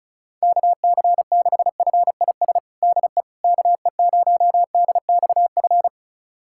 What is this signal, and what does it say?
Morse code, heard as a keyed tone, about 34 wpm, 700 Hz. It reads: KC6FIS DE KE0DXF
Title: USA Callsign DE Callsign - 6 Characters - 35wpm